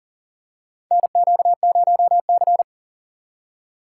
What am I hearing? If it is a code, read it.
Morse code, heard as a keyed tone, about 40 wpm, 700 Hz. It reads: NQ0C
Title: USA Call Signs - 4 Characters - 2x Character Spacing - 40wpm